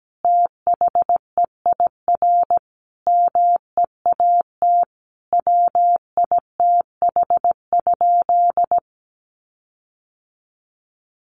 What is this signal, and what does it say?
Morse code, heard as a keyed tone, about 17 wpm, 700 Hz. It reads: THEIR MEAT WITH?
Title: Sentences from Top 1400 Words 17wpm